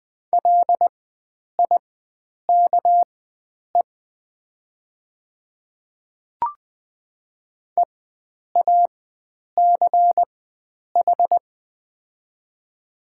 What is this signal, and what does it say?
Morse code, heard as a keyed tone, about 20 wpm, spaced as for 9 wpm, 700 Hz. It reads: LIKE EACH
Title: 4x Character Spacing - Top 100 Words - 20wpm